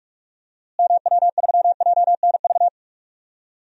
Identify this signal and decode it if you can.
Morse code, heard as a keyed tone, about 46 wpm, 700 Hz. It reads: MW3JNV
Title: DXCC-100 Call Signs - 6 Characters - Repeated 3x - 45wpm